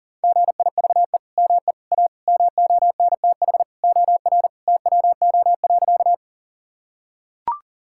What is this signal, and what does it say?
Morse code, heard as a keyed tone, about 40 wpm, 700 Hz. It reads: GIVE ME A MONTH OR TWO.
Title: Sentences from Top 1000 Words - Review Part 5 - 40wpm